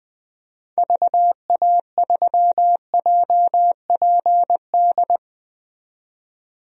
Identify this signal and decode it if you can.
Morse code, heard as a keyed tone, about 20 wpm, 700 Hz. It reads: VA3JPD
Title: DXCC-100 Call Signs - 6 Characters - 20wpm